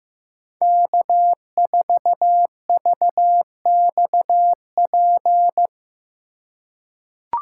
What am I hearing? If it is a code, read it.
Morse code, heard as a keyed tone, about 16 wpm, 700 Hz. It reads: K4VXP
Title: USA Call Signs - 5 Characters - 3x Character Spacing - 15wpm